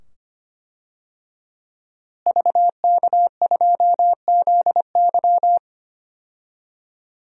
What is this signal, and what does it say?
Morse code, heard as a keyed tone, about 25 wpm, 700 Hz. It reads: VK2ZY
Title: DXCC-100 Call Signs - 5 Characters - Repeated 2x - 25wpm